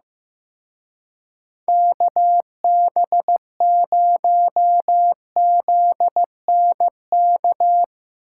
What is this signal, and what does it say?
Morse code, heard as a keyed tone, about 17 wpm, 700 Hz. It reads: KB0ZNK
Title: Speed Racing - USA 2x3 Call Signs 15wpm